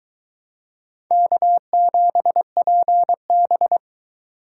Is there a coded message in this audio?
Yes: Morse code from 1.1 to 3.8 s, heard as a keyed tone, about 22 wpm, 700 Hz. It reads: K7PB